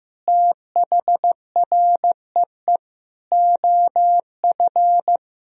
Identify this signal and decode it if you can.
Morse code, heard as a keyed tone, about 15 wpm, 700 Hz. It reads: THREE OF
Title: Sentences from Top 1000 Words - Review Part 1 - 15wpm